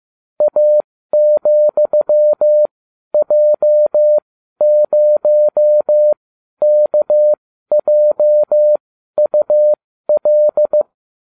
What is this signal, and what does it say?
Morse code, heard as a keyed tone, about 15 wpm, spaced as for 12 wpm, 600 Hz. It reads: A,J0KJUL